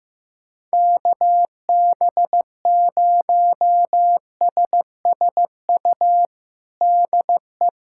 Morse code, heard as a keyed tone, about 15 wpm, 700 Hz. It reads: KB0SSU DE
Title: USA Callsign DE Callsign - 6 Characters - 15wpm